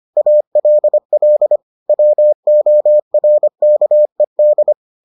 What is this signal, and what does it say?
Morse code, heard as a keyed tone, about 25 wpm, 600 Hz. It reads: ALL WORKED